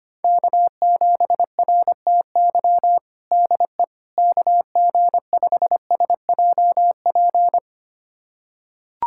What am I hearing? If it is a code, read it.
Morse code, heard as a keyed tone, about 25 wpm, 700 Hz. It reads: K7RTY DE KG5SJP
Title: USA Callsign DE Callsign 25wpm